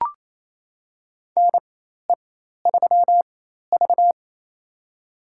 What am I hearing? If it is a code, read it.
Morse code, heard as a keyed tone, about 28 wpm, spaced as for 13 wpm, 700 Hz. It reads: NE3V
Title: USA Call Signs - 4 Characters - 4x Character Spacing - 28wpm